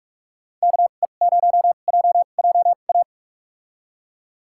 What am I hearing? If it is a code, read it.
Morse code, heard as a keyed tone, about 45 wpm, spaced as for 32 wpm, 700 Hz. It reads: KE0JJA
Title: USA Call Signs - 6 Characters - 2x Character Spacing - 45wpm